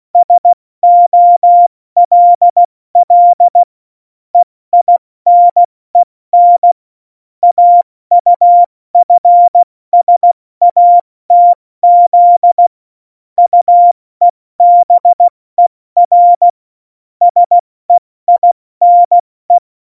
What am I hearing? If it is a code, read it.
Morse code, heard as a keyed tone, about 16 wpm, 700 Hz. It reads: SOLL EINEN AUFSATZ UEBER SEINE
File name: Morsekurs / Witze / Witz 19. Februar 2019, 70 F80